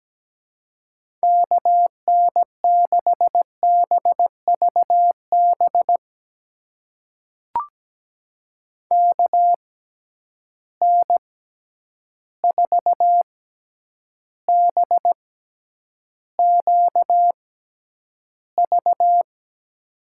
Morse code, heard as a keyed tone, about 17 wpm, spaced as for 9 wpm, 700 Hz. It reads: KN6BVB KN4BQV
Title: USA Call Signs - 6x Character Spacing - 17wpm